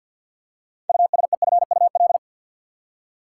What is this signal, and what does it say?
Morse code, heard as a keyed tone, about 51 wpm, 700 Hz. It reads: USEFUL
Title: Six Letter Words - 50wpm